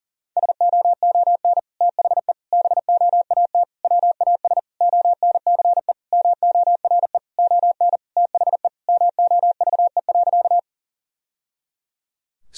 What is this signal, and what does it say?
Morse code, heard as a keyed tone, about 40 wpm, 700 Hz. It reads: SOON THE BOAT WAS ONCE MORE ON THE MOVE.
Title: Sentences from Top 1000 Words - Review Part 3 - 40wpm